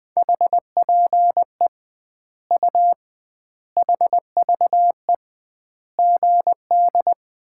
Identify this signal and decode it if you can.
Morse code, heard as a keyed tone, about 20 wpm, 700 Hz. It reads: HPE U HVE GD
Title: Common QSO Phrases - 2x Word-Spacing - 20wpm